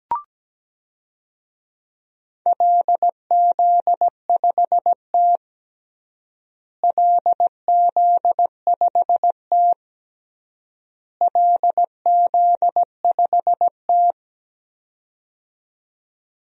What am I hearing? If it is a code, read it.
Morse code, heard as a keyed tone, about 17 wpm, 700 Hz. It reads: LZ5T LZ5T LZ5T
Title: DXCC-100 Call Signs - 4 Characters - Repeated 3x - 17wpm